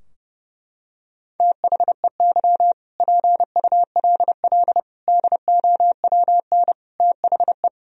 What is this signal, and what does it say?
Morse code, heard as a keyed tone, about 30 wpm, 700 Hz. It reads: THEY PULL DOWN THE
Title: Sentences from Top 1000 Words - Review Part 2 - 30wpm